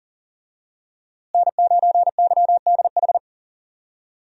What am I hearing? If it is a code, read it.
Morse code, heard as a keyed tone, about 40 wpm, 700 Hz. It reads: N9YDH
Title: USA Call Signs - 5 Characters - Repeated 3x - 40wpm